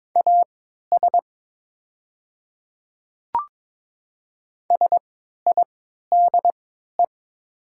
Morse code, heard as a keyed tone, about 22 wpm, spaced as for 13 wpm, 700 Hz. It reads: AS SIDE
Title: 3x Character Spacing - Top 100 Words - 22wpm